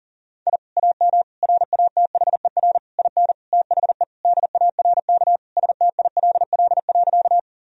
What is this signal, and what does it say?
Morse code, heard as a keyed tone, about 40 wpm, 700 Hz. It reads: I AM RATHER IN THE DARK STILL.
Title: Sentences from Top 1000 Words - Review Part 4 - 40wpm